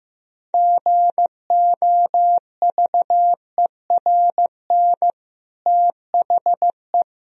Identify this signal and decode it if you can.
Morse code, heard as a keyed tone, about 15 wpm, 700 Hz. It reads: GOVERN THE
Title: Sentences from Top 1000 Words - Review Part 2 - 15wpm